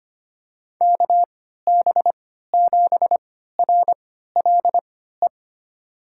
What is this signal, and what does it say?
Morse code, heard as a keyed tone, about 25 wpm, spaced as for 14 wpm, 700 Hz. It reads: KB7RLE
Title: USA Call Signs - 6 Characters - 3x Character Spacing - 25wpm